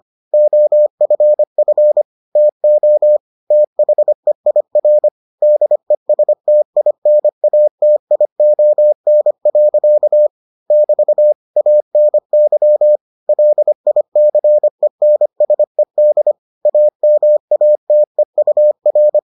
Morse code, heard as a keyed tone, about 25 wpm, 600 Hz. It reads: OFF TO THEIR DESTINATION. <BT> ANY LICENSED AMATEUR